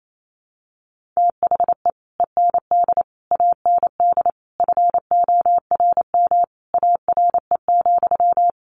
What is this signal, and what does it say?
Morse code, heard as a keyed tone, about 28 wpm, 700 Hz. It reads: THE END AND FORM ARE,
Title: Sentences from Top 1400 Words 28wpm